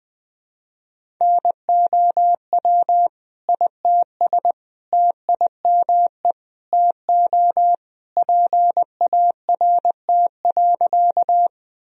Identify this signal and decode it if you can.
Morse code, heard as a keyed tone, about 20 wpm, 700 Hz. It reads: NOW ITS TIME TO PART.
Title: Sentences from Top 1000 Words - Review Part 1 - 20wpm